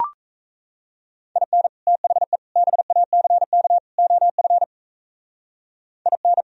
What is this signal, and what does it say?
Morse code, heard as a keyed tone, about 35 wpm, 700 Hz. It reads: IN THE BACK OF IN
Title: Speed-Racing - Sets of 4 Words 28wpm